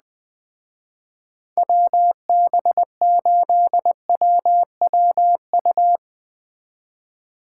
Morse code, heard as a keyed tone, about 20 wpm, 700 Hz. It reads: WB8WWU